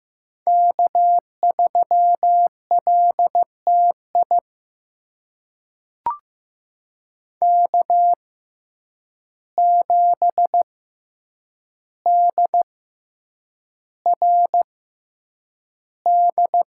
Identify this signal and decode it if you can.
Morse code, heard as a keyed tone, about 15 wpm, spaced as for 8 wpm, 700 Hz. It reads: K3LTI K7DRD
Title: USA Call Signs - 5 Characters - 6x Character Spacing - 15wpm